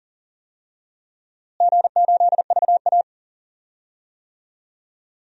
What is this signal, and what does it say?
Morse code, heard as a keyed tone, about 40 wpm, 700 Hz. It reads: G8VA
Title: DXCC-100 Call Signs - 4 Characters - Repeated 3x - 40wpm